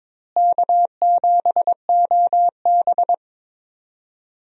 Morse code, heard as a keyed tone, about 22 wpm, 700 Hz. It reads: K7OB
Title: USA Call Signs - 4 Characters - Repeated 2x - 22wpm